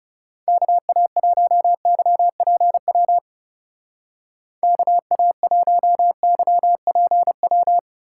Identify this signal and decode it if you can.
Morse code, heard as a keyed tone, about 36 wpm, 700 Hz. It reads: KA1YPW KA1YPW
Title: Speed Racing - USA 2x3 Call Signs 30wpm